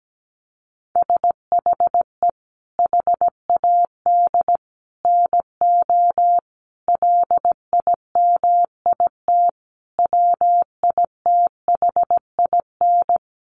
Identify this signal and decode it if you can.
Morse code, heard as a keyed tone, about 17 wpm, 700 Hz. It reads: SHE HAD NO LIMIT WITHIN
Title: Sentences from Top 1500 Words 17wpm